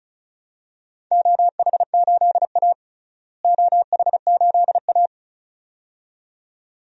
Morse code, heard as a keyed tone, about 36 wpm, 700 Hz. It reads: OH8A OH8A
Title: DXCC-100 Call Signs - 4 Characters - Repeated 2x - 35wpm